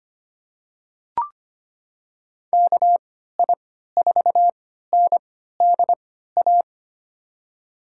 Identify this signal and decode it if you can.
Morse code, heard as a keyed tone, about 25 wpm, spaced as for 14 wpm, 700 Hz. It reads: KI4NDA
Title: USA Call Signs - 6 Characters - 3x Character Spacing - 25wpm